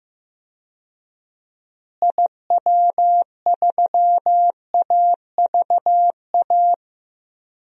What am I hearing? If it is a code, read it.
Morse code, heard as a keyed tone, about 15 wpm, 700 Hz. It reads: IW3AVA
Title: DXCC-100 Call Signs - 6 Characters - Repeated 2x - 15wpm